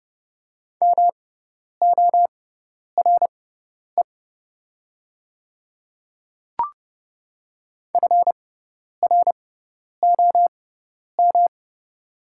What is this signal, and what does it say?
Morse code, heard as a keyed tone, about 30 wpm, spaced as for 10 wpm, 700 Hz. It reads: MORE FROM
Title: 6x Character Spacing - Top 100 Words - 30wpm